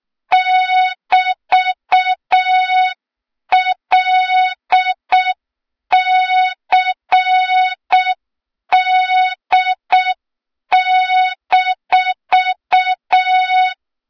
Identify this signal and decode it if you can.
Morse code, heard as a keyed tone, about 6 wpm, 3700 Hz. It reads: <BT>LCD-
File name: Morze-HA2MN / hang / 23 GYAKORLAT